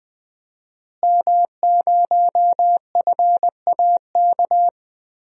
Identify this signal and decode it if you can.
Morse code, heard as a keyed tone, about 21 wpm, 700 Hz. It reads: M0FAK
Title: DXCC-100 Call Signs - 5 Characters - Repeated 3x - 20wpm